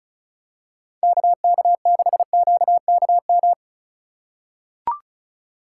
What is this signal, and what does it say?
Morse code, heard as a keyed tone, about 34 wpm, 700 Hz. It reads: KK6QKM
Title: USA Call Signs - 6 Characters - 4x Character Spacing - 35wpm